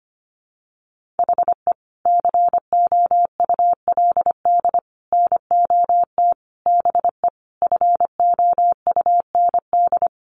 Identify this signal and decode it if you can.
Morse code, heard as a keyed tone, about 25 wpm, 700 Hz. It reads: HE COULD NOT BE FOUND
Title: Sentences from Top 1400 Words 25wpm